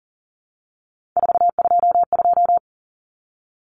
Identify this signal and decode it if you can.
Morse code, heard as a keyed tone, about 40 wpm, 700 Hz. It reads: 422
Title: Common Cut Numbers - 3-Digit 40wpm